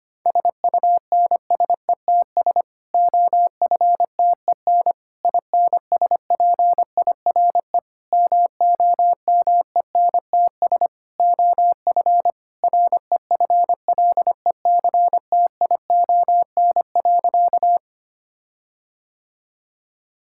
Morse code, heard as a keyed tone, about 25 wpm, 700 Hz. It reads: SUNSETS OFTEN INSPIRE MOMENTS OF REFLECTION.
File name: practice / 25wpm / Difficult Sentences with Hints - 25wpm